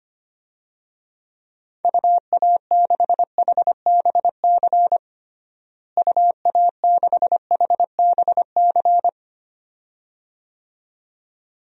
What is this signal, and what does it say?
Morse code, heard as a keyed tone, about 25 wpm, 700 Hz. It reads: UA6HBC UA6HBC
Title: DXCC-100 Call Signs - 6 Characters - Repeated 2x - 25wpm